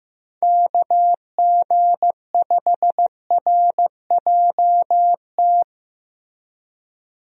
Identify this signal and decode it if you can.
Morse code, heard as a keyed tone, about 15 wpm, 700 Hz. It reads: KG5RJT
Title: USA Call Signs - 6 Characters - Repeated 2x - 15wpm